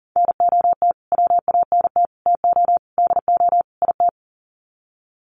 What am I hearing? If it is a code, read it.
Morse code, heard as a keyed tone, about 37 wpm, 700 Hz. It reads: NOT WANT TO DO IT
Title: Sets of 5 Words Encore 40wpm